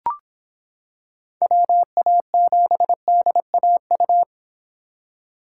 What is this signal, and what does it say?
Morse code, heard as a keyed tone, about 26 wpm, 700 Hz. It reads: WA7DAU